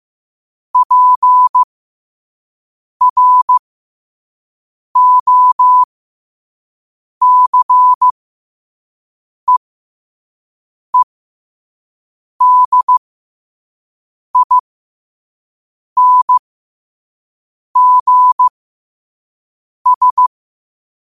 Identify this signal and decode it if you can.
Morse code, heard as a keyed tone, about 15 wpm, spaced as for 5 wpm, 1000 Hz. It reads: PROCEEDINGS
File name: qotdfiles / 5wpm